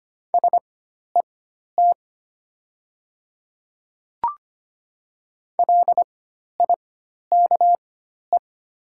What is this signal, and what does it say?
Morse code, heard as a keyed tone, about 25 wpm, spaced as for 12 wpm, 690 Hz. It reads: SET LIKE
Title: 4x Character Spacing - Top 500 Words - 25wpm